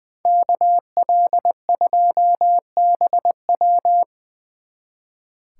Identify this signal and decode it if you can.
Morse code, heard as a keyed tone, about 25 wpm, 700 Hz. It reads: KL2BW
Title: Speed Racing - USA 2x2 Call Signs 20wpm